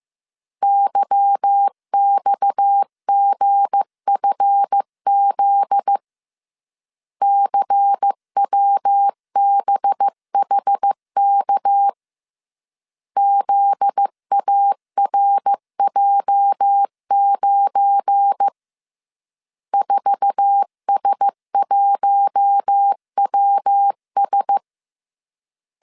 Morse code, heard as a keyed tone, about 15 wpm, 800 Hz. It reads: YXGFZ CWBHK ZARJ9 4S1WS
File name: CW / DET / 1 9 1500C S A 14wpm 3222